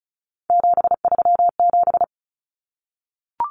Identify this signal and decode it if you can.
Morse code, heard as a keyed tone, about 34 wpm, 700 Hz. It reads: 737
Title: Numbers - 3-Digit 35wpm